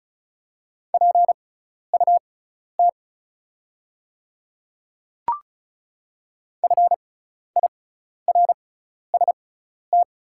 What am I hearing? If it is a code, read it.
Morse code, heard as a keyed tone, about 36 wpm, spaced as for 12 wpm, 690 Hz. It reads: PUT FIRST
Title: 6x Character Spacing - Top 500 Words - 35wpm